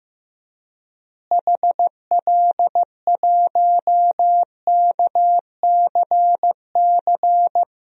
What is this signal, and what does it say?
Morse code, heard as a keyed tone, about 15 wpm, 700 Hz. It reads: HL1KCC
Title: DXCC-100 Call Signs - 6 Characters - Repeated 3x - 15wpm